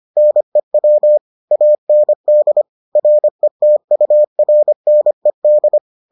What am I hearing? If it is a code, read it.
Morse code, heard as a keyed tone, about 25 wpm, 600 Hz. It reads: NEW AND RETURNED